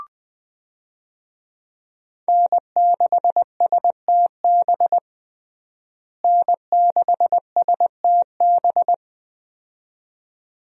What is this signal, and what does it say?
Morse code, heard as a keyed tone, about 21 wpm, 700 Hz. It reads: N6STB N6STB
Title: USA Call Signs - 5 Characters - Repeated 2x - 20wpm